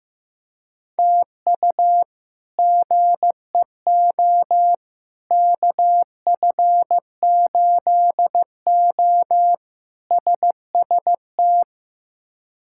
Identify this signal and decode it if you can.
Morse code, heard as a keyed tone, about 16 wpm, 700 Hz. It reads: TU GEO KF8O SST
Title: K1USN SST QSO Practice - 15wpm